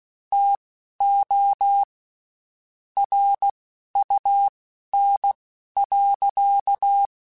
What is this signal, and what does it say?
Morse code, heard as a keyed tone, about 16 wpm, spaced as for 12 wpm, 800 Hz. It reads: TO RUN.
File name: (spaced) / QSOs Spaced-mp3s / kc7sac @15 sp